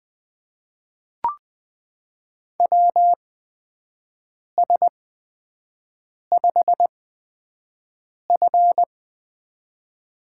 Morse code, heard as a keyed tone, about 20 wpm, spaced as for 5 wpm, 700 Hz. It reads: WS5F
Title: USA Call Signs - 4 Characters - 8x Character Spacing - 20wpm